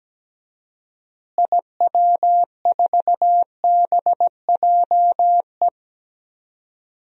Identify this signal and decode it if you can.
Morse code, heard as a keyed tone, about 17 wpm, 700 Hz. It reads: IW4BJE
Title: DXCC-100 Call Signs - 6 Characters - Repeated 2x - 17wpm